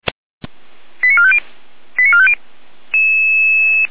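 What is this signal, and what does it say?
Five-tone selective call (ZVEI), heard as quick runs of five tones, tones of about 70 ms. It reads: 89450 89450
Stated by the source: ZVEI five-tone selective call received on SDR.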